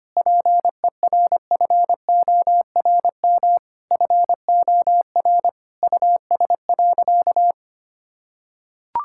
Morse code, heard as a keyed tone, about 25 wpm, 700 Hz. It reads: PERFORM FOR US.
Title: Sentences from Top 1000 Words - Review Part 5 - 25wpm